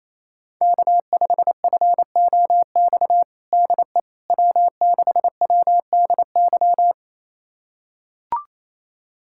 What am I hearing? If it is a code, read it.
Morse code, heard as a keyed tone, about 27 wpm, 700 Hz. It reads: K5FOX DE W6WDY
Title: USA Callsign DE Callsign - 5 Characters - 28wpm